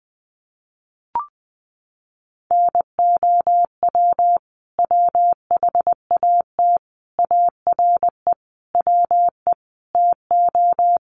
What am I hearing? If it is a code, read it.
Morse code, heard as a keyed tone, about 20 wpm, 700 Hz. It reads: NOW WHAT ARE WE TO
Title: Sentences from Top 100 Words 20wpm